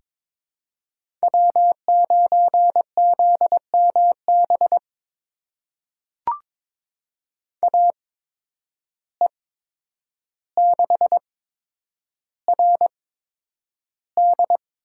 Morse code, heard as a keyed tone, about 21 wpm, spaced as for 9 wpm, 700 Hz. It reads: W9ZMB AE6RD
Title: USA Call Signs - 5 Characters - 8x Character Spacing - 22wpm